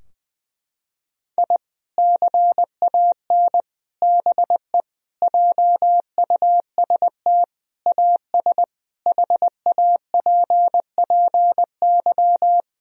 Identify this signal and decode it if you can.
Morse code, heard as a keyed tone, about 20 wpm, 700 Hz. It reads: I CAN BE JUST AS HAPPY
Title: Sentences from Top 1000 Words - Review Part 3 - 20wpm